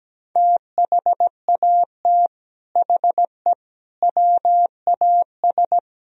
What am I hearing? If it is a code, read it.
Morse code, heard as a keyed tone, about 17 wpm, 700 Hz. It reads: THAT HE WAS
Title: Speed-Racing - Sets of 5 Words 17wpm